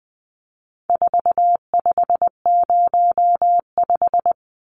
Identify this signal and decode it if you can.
Morse code, heard as a keyed tone, about 19 wpm, 700 Hz. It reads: 4505
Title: Numbers - 4-Digit 20wpm